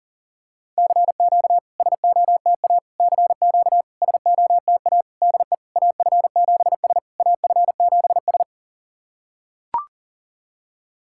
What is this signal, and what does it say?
Morse code, heard as a keyed tone, about 40 wpm, 690 Hz. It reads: CQ SOTA CQ SOTA DE AF7S AF7S
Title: SOTA QSO Practice 40wpm